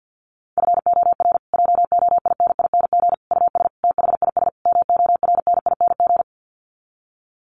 Morse code, heard as a keyed tone, about 50 wpm, 700 Hz. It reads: FOR JOINING US THIS MORNING
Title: Sets of 5 Words Encore 50wpm